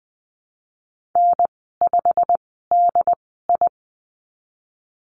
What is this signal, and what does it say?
Morse code, heard as a keyed tone, about 20 wpm, spaced as for 15 wpm, 700 Hz. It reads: N5DI
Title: 088 Lesson 15g - Number 5 - Call Signs 20wpm